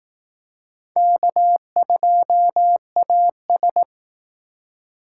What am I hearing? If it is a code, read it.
Morse code, heard as a keyed tone, about 18 wpm, 700 Hz. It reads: K2AS
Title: Speed Racing - USA 1x2 Call Signs 15wpm